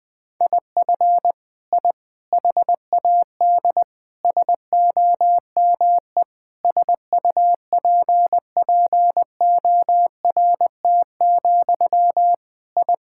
Morse code, heard as a keyed tone, about 21 wpm, 700 Hz. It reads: IF I HAD SOME SUPPORT, I
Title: Sentences from Top 1000 Words - Review Part 5 - 20wpm